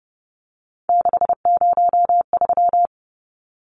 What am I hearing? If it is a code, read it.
Morse code, heard as a keyed tone, about 30 wpm, 700 Hz. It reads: 603
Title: Numbers - 3-Digit 30wpm